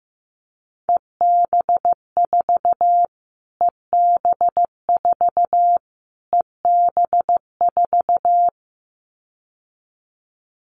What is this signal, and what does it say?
Morse code, heard as a keyed tone, about 15 wpm, 700 Hz. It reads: EB4 EB4 EB4
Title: Callsign Prefixes for Top 25 DXCC - Repeated 3x - 15wpm